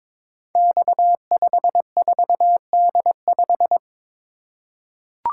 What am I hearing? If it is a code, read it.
Morse code, heard as a keyed tone, about 23 wpm, 700 Hz. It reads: X54D5
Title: Code Groups - 5 Letter:Numbers - 22wpm